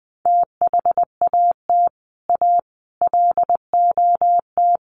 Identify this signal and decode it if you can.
Morse code, heard as a keyed tone, about 20 wpm, 700 Hz. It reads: THAT A LOT
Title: Sets of 5 Words Encore2 20wpm